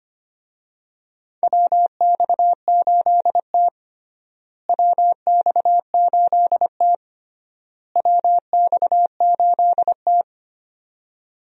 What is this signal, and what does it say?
Morse code, heard as a keyed tone, about 25 wpm, 700 Hz. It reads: WX8T WX8T WX8T
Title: USA Call Signs - 4 Characters - Repeated 3x - 25wpm